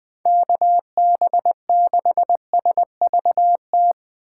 Morse code, heard as a keyed tone, about 20 wpm, 700 Hz. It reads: KB6SVT
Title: USA Call Signs - 6 Characters - 6x Character Spacing - 20wpm